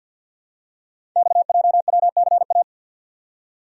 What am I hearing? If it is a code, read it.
Morse code, heard as a keyed tone, about 52 wpm, 700 Hz. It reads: XJWCA
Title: Code Groups - 5 Letter:Numbers - 50wpm